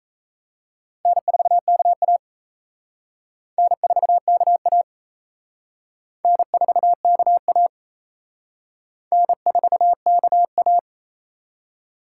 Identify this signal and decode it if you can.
Morse code, heard as a keyed tone, about 37 wpm, 700 Hz. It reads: N4KA N4KA N4KA N4KA